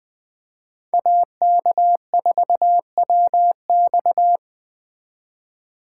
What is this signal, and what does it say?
Morse code, heard as a keyed tone, about 20 wpm, 700 Hz. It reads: AK4WX
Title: Speed Racing - USA 2x2 Call Signs 17wpm